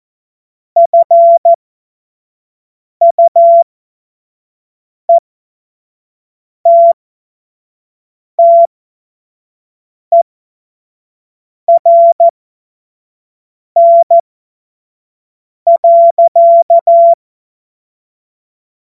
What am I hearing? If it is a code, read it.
Morse code, heard as a keyed tone, about 14 wpm, spaced as for 5 wpm, 680 Hz. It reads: FUETTERN.
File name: Morsekurs / Witze / Witz 07. September 2021, 25 F70